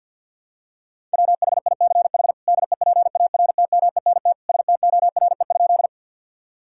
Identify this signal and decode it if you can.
Morse code, heard as a keyed tone, about 52 wpm, 700 Hz. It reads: WHICH DEPARTMENT STORE?
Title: Sentences from Top 1000 Words - Review Part 5 - 50wpm